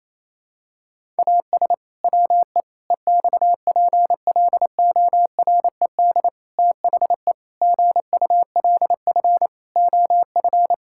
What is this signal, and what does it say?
Morse code, heard as a keyed tone, about 28 wpm, 700 Hz. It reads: AS WE EXPLORED THE GULF OF
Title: Pangram Challenge v2 - 28wpm